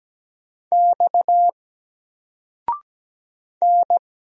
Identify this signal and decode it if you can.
Morse code, heard as a keyed tone, about 17 wpm, 700 Hz. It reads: X N
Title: Single Letter-Number 17wpm